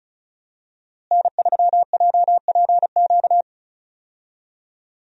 Morse code, heard as a keyed tone, about 34 wpm, 700 Hz. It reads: N3JPQ
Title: USA Call Signs - 5 Characters - Repeated 2x - 35wpm